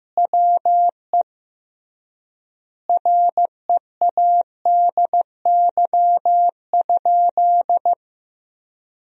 Morse code, heard as a keyed tone, about 15 wpm, 700 Hz. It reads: WE READY?
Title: 3x Word-Spacing - Sentences from Top 400 Words - 15wpm